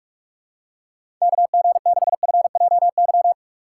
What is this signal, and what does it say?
Morse code, heard as a keyed tone, about 47 wpm, 700 Hz. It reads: KG6FJY
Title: USA Call Signs - 6 Characters - 2x Character Spacing - 45wpm